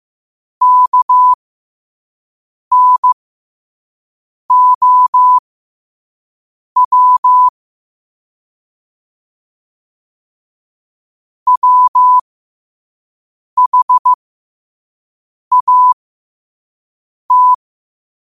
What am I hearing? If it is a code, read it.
Morse code, heard as a keyed tone, about 15 wpm, spaced as for 5 wpm, 1000 Hz. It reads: KNOW WHAT
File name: qotdfiles / 5wpm